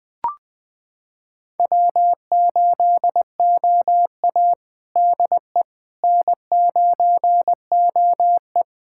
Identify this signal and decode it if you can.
Morse code, heard as a keyed tone, about 20 wpm, 700 Hz. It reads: W8OA DE N9OE
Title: USA Callsign DE Callsign - 4 Characters - 20wpm